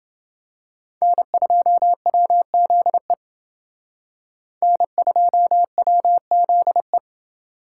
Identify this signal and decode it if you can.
Morse code, heard as a keyed tone, about 28 wpm, 700 Hz. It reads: N2WZE N2WZE